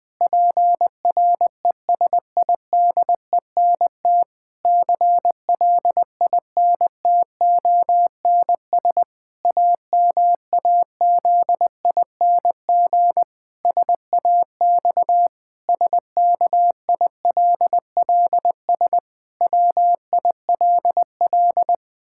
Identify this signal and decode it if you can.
Morse code, heard as a keyed tone, about 20 wpm, 700 Hz. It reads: PRESIDENT CLINTONS AMAZING SAX SKILLS WILL
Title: Pangram Challenge v2 - 20wpm